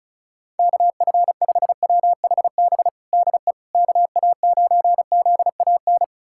Morse code, heard as a keyed tone, about 34 wpm, 700 Hz. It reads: KF5WHB DE KA9ZAN
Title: USA Callsign DE Callsign - 6 Characters - 35wpm